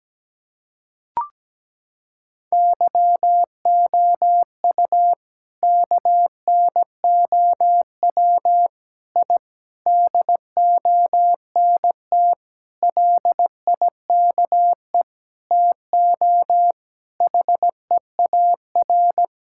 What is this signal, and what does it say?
Morse code, heard as a keyed tone, about 17 wpm, 700 Hz. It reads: YOU KNOW I DONT LIKE TO HEAR
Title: Sentences from Top 1000 Words - Review Part 6 - 17wpm